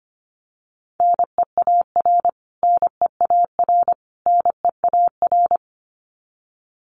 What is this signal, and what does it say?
Morse code, heard as a keyed tone, about 25 wpm, 700 Hz. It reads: NEAR NEAR NEAR
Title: Top 500 Words Repeated 25wpm